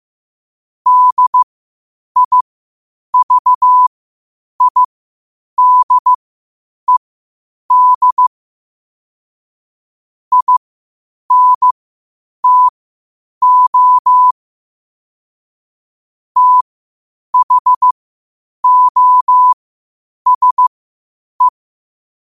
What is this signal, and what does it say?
Morse code, heard as a keyed tone, about 15 wpm, spaced as for 8 wpm, 1000 Hz. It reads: DIVIDED INTO THOSE
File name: qotdfiles / 8wpm